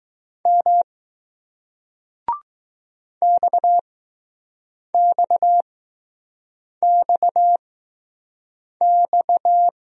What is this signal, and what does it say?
Morse code, heard as a keyed tone, about 20 wpm, spaced as for 6 wpm, 700 Hz. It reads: M XXXX